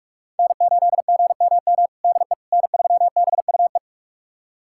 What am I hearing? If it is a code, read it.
Morse code, heard as a keyed tone, about 46 wpm, 700 Hz. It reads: N8GMM DE N3BUE
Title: USA Callsign DE Callsign - 5 Characters - 45wpm